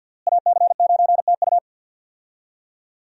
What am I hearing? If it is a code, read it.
Morse code, heard as a keyed tone, about 47 wpm, 700 Hz. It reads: AC9TU
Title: USA Call Signs - 5 Characters - Repeated 2x - 50wpm